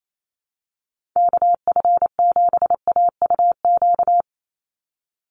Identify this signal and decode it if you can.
Morse code, heard as a keyed tone, about 28 wpm, 700 Hz. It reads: KF7AUQ
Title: USA Call Signs 28wpm